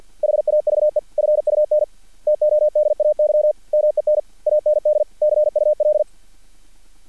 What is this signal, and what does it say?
Morse code, heard as a keyed tone, about 50 wpm, 600 Hz. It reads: HAVE FUN TODAY MEN AND CUL
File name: archived / CRSnet